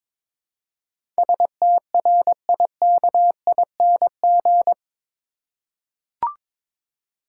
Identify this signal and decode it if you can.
Morse code, heard as a keyed tone, about 22 wpm, 700 Hz. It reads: STRIKING
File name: practice / 22wpm / Eight Letter Words - 22wpm